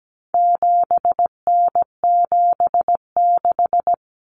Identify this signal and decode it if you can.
Morse code, heard as a keyed tone, about 17 wpm, 700 Hz. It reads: 7N76
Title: Common Cut Numbers - 4-Digit 17wpm